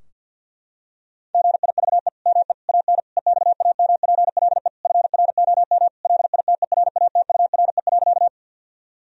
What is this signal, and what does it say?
Morse code, heard as a keyed tone, about 52 wpm, 700 Hz. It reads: GIVE ME AN EXAMPLE FROM LITERATURE.
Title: Sentences from Top 1000 Words - Review Part 3 - 50wpm